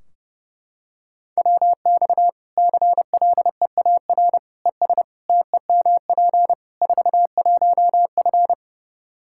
Morse code, heard as a keyed tone, about 32 wpm, 700 Hz. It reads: WX CLEAR ES TEMP 41F
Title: K1USN SST QSO Practice 30wpm